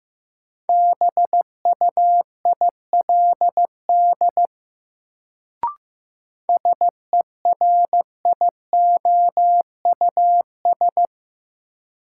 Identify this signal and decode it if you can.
Morse code, heard as a keyed tone, about 15 wpm, 700 Hz. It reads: BUILD SERIOUS
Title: Sentences from Top 1100 Words - New Word Drill Alt - 15wpm